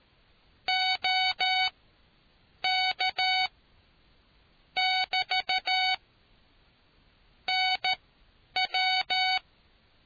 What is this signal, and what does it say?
Morse code, heard as a keyed tone, about 13 wpm, spaced as for 8 wpm, 750 Hz. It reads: OK <BT> NW